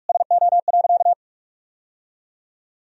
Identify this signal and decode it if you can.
Morse code, heard as a keyed tone, about 46 wpm, 700 Hz. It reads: SO.
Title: Sentences from Top 1000 Words - Review Part 1 - 45wpm